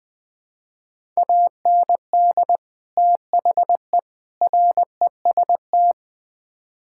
Morse code, heard as a keyed tone, about 20 wpm, 700 Hz. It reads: AND THE REST
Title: Build A Phrase - 4-Word Phrases - 20wpm